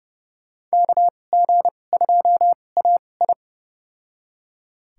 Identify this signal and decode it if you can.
Morse code, heard as a keyed tone, about 30 wpm, spaced as for 22 wpm, 700 Hz. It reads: KG2AI
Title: USA Call Signs - 5 Characters - 2x Character Spacing - 30wpm